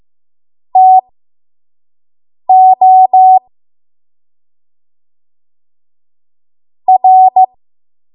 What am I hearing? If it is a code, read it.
Morse code, heard as a keyed tone, about 15 wpm, spaced as for 5 wpm, 750 Hz. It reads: TO R
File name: Morse / 05WPM